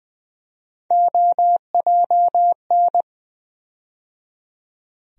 Morse code, heard as a keyed tone, about 20 wpm, 700 Hz. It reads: OJN
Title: USA Call Sign Postfixes - 3 Letters 20wpm